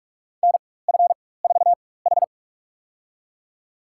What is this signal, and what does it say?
Morse code, heard as a keyed tone, about 46 wpm, spaced as for 21 wpm, 700 Hz. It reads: NF4H
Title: USA Call Signs - 4x Character Spacing - 45wpm